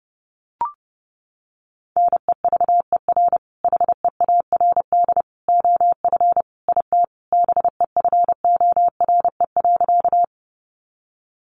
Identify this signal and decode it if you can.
Morse code, heard as a keyed tone, about 30 wpm, 700 Hz. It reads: NEVER HEARD OF IT BEFORE.
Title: Sentences from Top 400 Words 30wpm